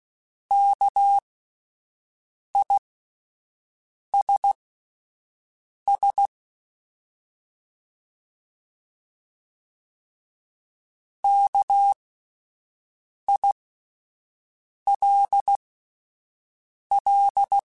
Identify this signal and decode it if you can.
Morse code, heard as a keyed tone, about 16 wpm, spaced as for 5 wpm, 800 Hz. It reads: KISS KILL
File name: Alphabetical / Alphabetical mp3s / Ks @5